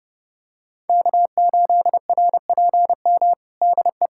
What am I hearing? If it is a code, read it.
Morse code, heard as a keyed tone, about 30 wpm, 700 Hz. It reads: K8RPM DE
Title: USA Callsign DE Callsign - 5 Characters - 30wpm